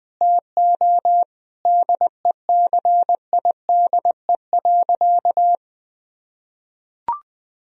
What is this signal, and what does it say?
Morse code, heard as a keyed tone, about 20 wpm, 700 Hz. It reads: TO DECIDE.
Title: Sentences from Top 1000 Words - Review Part 3 - 20wpm